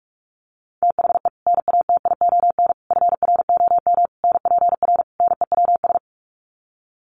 Morse code, heard as a keyed tone, about 44 wpm, 700 Hz. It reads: THE NATION FROM NPR NEWS
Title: Sets of 5 Words 45wpm